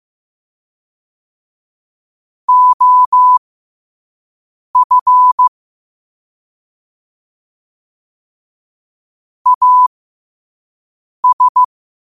Morse code, heard as a keyed tone, about 15 wpm, spaced as for 5 wpm, 1000 Hz. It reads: OF AS
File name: qotdfiles / 5wpm